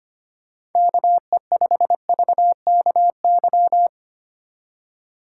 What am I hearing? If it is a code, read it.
Morse code, heard as a keyed tone, about 31 wpm, 700 Hz. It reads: KE5VKY
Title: Speed Racing - USA 2x3 Call Signs 25wpm